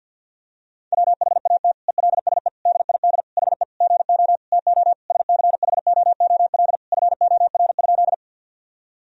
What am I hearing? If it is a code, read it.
Morse code, heard as a keyed tone, about 50 wpm, 700 Hz. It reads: WHAT ELSE DID HE GO TO SCHOOL FOR?
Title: Sentences from Top 1000 Words - Review Part 1 - 50wpm